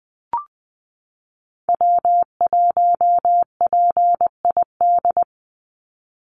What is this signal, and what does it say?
Morse code, heard as a keyed tone, about 20 wpm, 700 Hz. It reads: W1PID
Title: Call Signs Repeated 3x - 20wpm